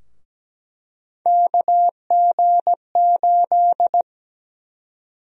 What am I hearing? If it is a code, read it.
Morse code, heard as a keyed tone, about 17 wpm, 700 Hz. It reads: KG8
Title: USA Call Sign Prefixes - Repeated 2x - 17wpm